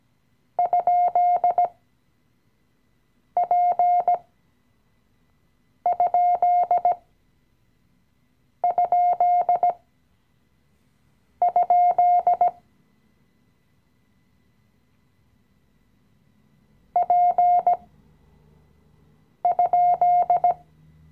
Morse code, heard as a keyed tone, about 17 wpm, spaced as for 5 wpm, 710 Hz. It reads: ?P??? P?
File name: CW-Online-Kurs / Anfänger / Lektion 018